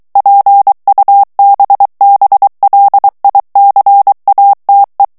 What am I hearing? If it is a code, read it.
Morse code, heard as a keyed tone, about 23 wpm, 800 Hz. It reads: PUBBLICATE
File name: ITALIEN texte clair 2 vitesse 22 mots minute